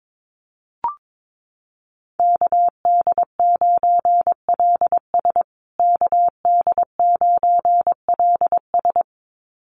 Morse code, heard as a keyed tone, about 22 wpm, 700 Hz. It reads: KD9LS KD9LS
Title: USA Call Signs Repeated 2x 22wpm